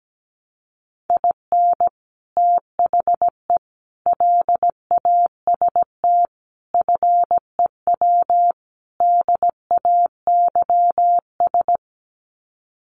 Morse code, heard as a keyed tone, about 17 wpm, 700 Hz. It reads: IN THE LAST FEW DAYS
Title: Sets of 5 Words Encore2 17wpm